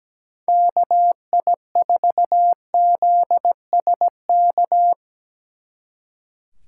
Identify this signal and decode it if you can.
Morse code, heard as a keyed tone, about 19 wpm, 700 Hz. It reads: KI4ZSK
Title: Speed Racing - USA 2x3 Call Signs 17wpm